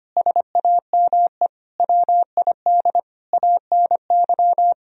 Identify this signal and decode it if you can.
Morse code, heard as a keyed tone, about 25 wpm, 700 Hz. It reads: SAME WID ANY
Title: Sentences from Top 300 Words and Abbreviations 25wpm